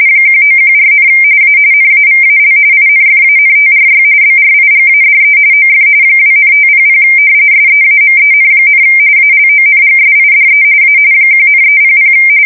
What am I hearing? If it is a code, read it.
RTTY, about 45.45 baud. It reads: 1914, A NAVAL INFANTRY BRIGADE OF 2,500 SEAMEN, ALONGSIDE A 1,300
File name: fsk170 / usnavy4